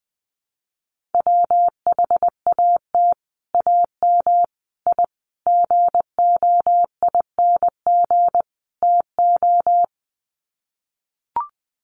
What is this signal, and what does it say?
Morse code, heard as a keyed tone, about 20 wpm, 700 Hz. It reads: WHAT AM I GOING TO
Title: Sets of 5 Words Encore 20wpm